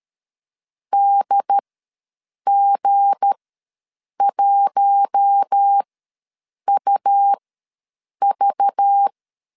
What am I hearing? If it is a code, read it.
Morse code, heard as a keyed tone, about 13 wpm, spaced as for 7 wpm, 800 Hz. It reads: DG1UV
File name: LEZIONI / 9 28 ws 12wpm 3354